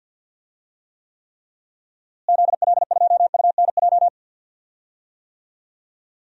Morse code, heard as a keyed tone, about 52 wpm, 700 Hz. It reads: ZL2UNJ
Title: DXCC-100 Call Signs - 6 Characters - Repeated 3x - 50wpm